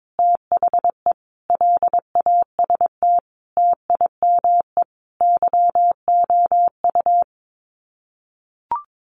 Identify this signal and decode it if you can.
Morse code, heard as a keyed tone, about 22 wpm, 700 Hz. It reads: THE LAST TIME YOU
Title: Sets of 5 Words Encore2 22wpm